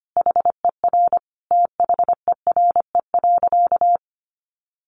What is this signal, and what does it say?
Morse code, heard as a keyed tone, about 25 wpm, 700 Hz. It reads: HER THERE.
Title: Sentences from Top 1500 Words 25wpm